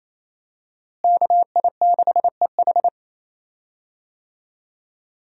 Morse code, heard as a keyed tone, about 27 wpm, 700 Hz. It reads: KI6EH
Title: USA Call Signs - 5 Characters - Repeated 3x - 28wpm